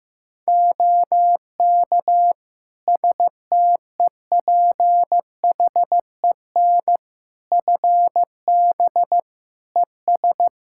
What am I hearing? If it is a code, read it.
Morse code, heard as a keyed tone, about 15 wpm, 700 Hz. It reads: OK STEPHEN FB ES
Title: Practice QSO - Exchange 2 - Info - 15wpm